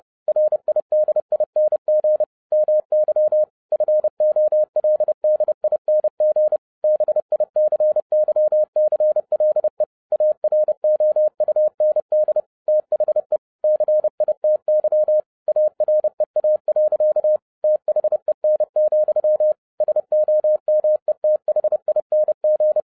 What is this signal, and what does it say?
Morse code, heard as a keyed tone, about 30 wpm, 610 Hz. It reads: RIDING MY FOLDING BICYCLE AROUND THE CITY AREA. THEN, SOMETHING
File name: 2011-07-mp3 / n55 BACK FROM A BIG ACCIDENT 30wpm